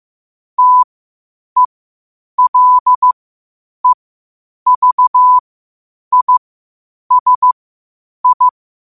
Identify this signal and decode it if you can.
Morse code, heard as a keyed tone, about 15 wpm, spaced as for 8 wpm, 1000 Hz. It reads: TELEVISI
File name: qotdfiles / 8wpm